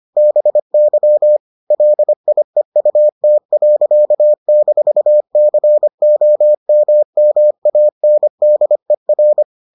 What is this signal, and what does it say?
Morse code, heard as a keyed tone, about 25 wpm, 600 Hz. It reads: BY LIEUT.-COMMANDER